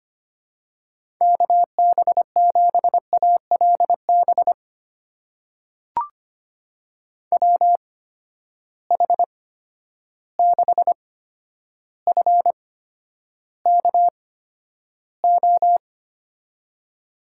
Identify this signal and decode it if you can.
Morse code, heard as a keyed tone, about 25 wpm, spaced as for 11 wpm, 700 Hz. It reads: KB7ALB WH6FKO
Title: USA Call Signs - 6 Characters - 8x Character Spacing - 25wpm